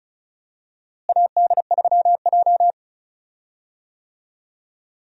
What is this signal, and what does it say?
Morse code, heard as a keyed tone, about 36 wpm, 700 Hz. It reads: AD3J
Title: USA Call Signs - 4 Characters - Repeated 3x - 35wpm